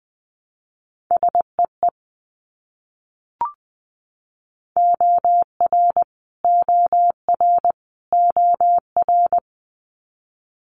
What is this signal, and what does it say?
Morse code, heard as a keyed tone, about 20 wpm, 700 Hz. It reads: SEE OR OR OR
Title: Multi-letter Phonemes Repeated 3x 20wpm